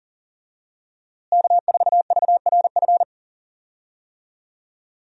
Morse code, heard as a keyed tone, about 39 wpm, 700 Hz. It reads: K4VRF
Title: USA Call Signs - 5 Characters - Repeated 2x - 40wpm